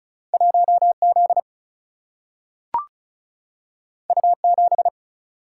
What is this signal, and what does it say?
Morse code, heard as a keyed tone, about 36 wpm, 700 Hz. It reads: 1Z U7
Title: Single-Letter Single-Number 35wpm